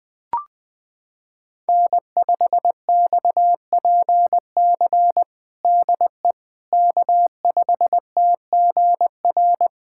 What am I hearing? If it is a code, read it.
Morse code, heard as a keyed tone, about 21 wpm, 700 Hz. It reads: N5XPC DE K5TGR
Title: USA Callsign DE Callsign - 5 Characters - 20wpm